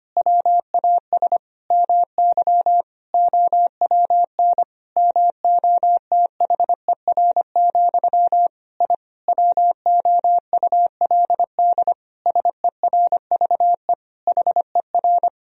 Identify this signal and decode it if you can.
Morse code, heard as a keyed tone, about 25 wpm, 700 Hz. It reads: WAS MY OWN MOTHER, I WOULD SERVE HER
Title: Sentences from Top 1000 Words - Review Part 2 - 25wpm